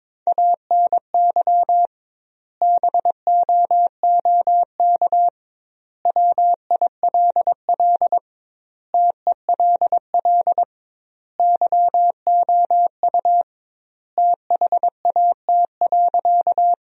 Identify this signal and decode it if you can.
Morse code, heard as a keyed tone, about 22 wpm, 700 Hz. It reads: ANY BOOK WILL TELL YOU THAT.
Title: 2x Word-Spacing - Sentences from Top 300 Words - 22wpm